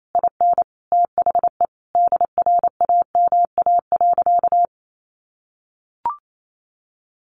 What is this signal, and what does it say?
Morse code, heard as a keyed tone, about 28 wpm, 700 Hz. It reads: IN THE DRAMA.
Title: Sentences from Top 1500 Words 28wpm